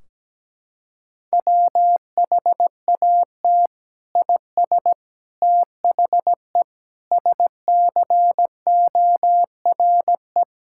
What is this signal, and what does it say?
Morse code, heard as a keyed tone, about 17 wpm, 700 Hz. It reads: WHAT IS THE SCORE
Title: Sentences from Top 1600 Words 17wpm